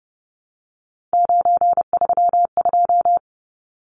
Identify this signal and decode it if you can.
Morse code, heard as a keyed tone, about 30 wpm, 700 Hz. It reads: 932
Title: Numbers - 3-Digit 30wpm